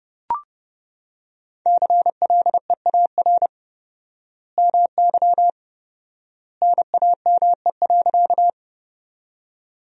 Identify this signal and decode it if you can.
Morse code, heard as a keyed tone, about 30 wpm, 700 Hz. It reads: CLEAR MY NAME.
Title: 4x Word-Spacing - Sentences from Top 500 Words - 30wpm